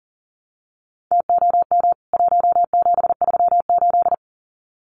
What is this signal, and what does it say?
Morse code, heard as a keyed tone, about 38 wpm, 700 Hz. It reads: TOM 1738
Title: CWT Exchange 40wpm